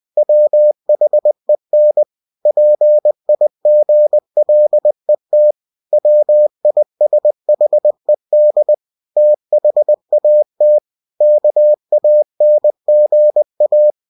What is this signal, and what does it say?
Morse code, heard as a keyed tone, about 20 wpm, 600 Hz. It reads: WHEN PIGLET WISHED THAT KANGA
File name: cw-books / winnie-the-pooh / winnie-20wpm-page-0070